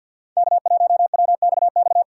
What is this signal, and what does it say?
Morse code, heard as a keyed tone, about 53 wpm, 700 Hz. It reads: K1WXX
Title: USA Call Signs - 5 Characters - 6x Character Spacing - 50wpm